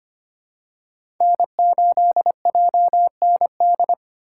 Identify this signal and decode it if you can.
Morse code, heard as a keyed tone, about 25 wpm, 700 Hz. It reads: N8JND
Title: USA Call Signs - 5 Characters - 8x Character Spacing - 25wpm